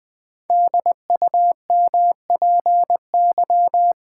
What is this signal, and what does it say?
Morse code, heard as a keyed tone, about 20 wpm, 700 Hz. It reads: DUMPY
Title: Pangram Challenge v2 - 20wpm